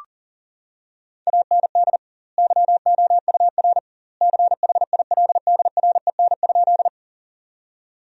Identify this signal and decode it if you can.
Morse code, heard as a keyed tone, about 40 wpm, 700 Hz. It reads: AND YOUR CHILDREN?
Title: 2x Word-Spacing - Sentences from Top 300 Words - 40wpm